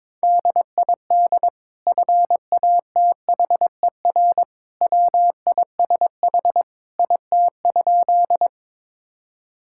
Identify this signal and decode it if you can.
Morse code, heard as a keyed tone, about 22 wpm, 700 Hz. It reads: DID FATHER WISH IT?
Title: Sentences from Top 1600 Words 22wpm